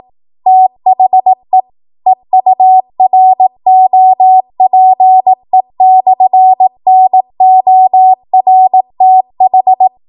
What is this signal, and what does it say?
Morse code, heard as a keyed tone, about 18 wpm, 750 Hz. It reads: THE EUROPE/NORTH